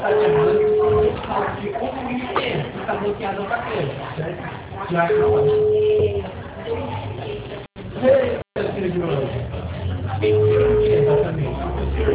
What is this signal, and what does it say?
Ringback tone (Europe, 425 Hz), with speech.